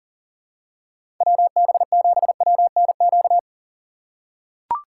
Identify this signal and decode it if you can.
Morse code, heard as a keyed tone, about 40 wpm, 700 Hz. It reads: WB7WNQ
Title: USA Call Signs - 6 Characters - 2x Character Spacing - 40wpm